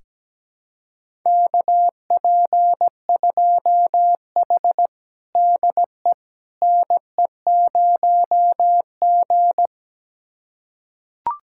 Morse code, heard as a keyed tone, about 17 wpm, 700 Hz. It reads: KP2H DE NE0G
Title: USA Callsign DE Callsign - 4 Characters - 17wpm